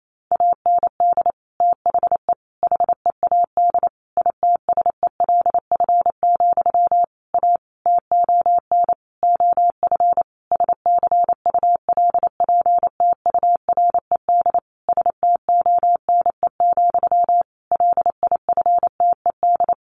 Morse code, heard as a keyed tone, about 28 wpm, 700 Hz. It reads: AND THE HEAD ITSELF, A TON OF SCULPTURED STONE, LIFTED
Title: A Sound Of Thunder 28wpm